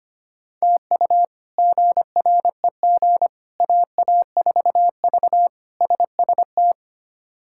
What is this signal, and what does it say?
Morse code, heard as a keyed tone, about 25 wpm, 700 Hz. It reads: TU GREG AA4V SST
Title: K1USN SST QSO Practice - 25wpm